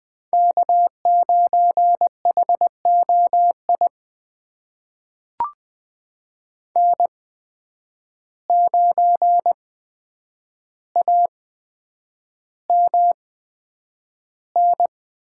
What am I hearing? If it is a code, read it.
Morse code, heard as a keyed tone, about 20 wpm, spaced as for 9 wpm, 700 Hz. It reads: K9HOI N9AMN
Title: USA Call Signs - 8x Character Spacing - 20wpm